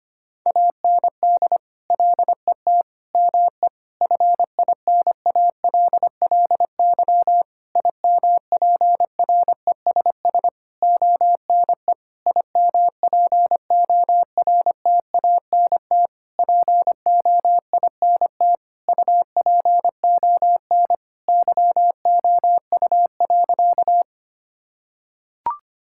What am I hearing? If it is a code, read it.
Morse code, heard as a keyed tone, about 25 wpm, 700 Hz. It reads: AND LET ME FINALLY IMPRESS ONE IMPORTANT POINT UPON YOU.
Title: Sentences from Top 1600 Words 25wpm